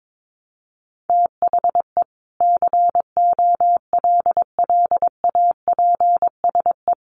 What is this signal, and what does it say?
Morse code, heard as a keyed tone, about 22 wpm, 700 Hz. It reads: THE COLLAPSE
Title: Sets of 5 Words Encore 22wpm